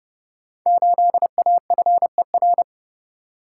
Morse code, heard as a keyed tone, about 30 wpm, 700 Hz. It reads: 8AFER
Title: Code Groups - 5 Letter:Numbers - 30wpm